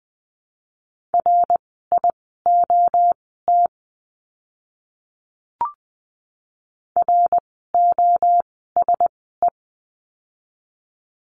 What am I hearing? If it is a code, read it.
Morse code, heard as a keyed tone, about 20 wpm, spaced as for 14 wpm, 700 Hz. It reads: RIOT ROSE
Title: 061 Lesson 11f - Letter h - Words 20wpm